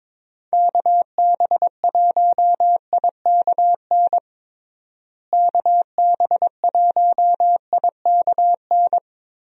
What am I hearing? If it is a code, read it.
Morse code, heard as a keyed tone, about 22 wpm, 700 Hz. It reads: KB1IKN KB1IKN
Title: USA Call Signs - 6 Characters - Repeated 2x - 22wpm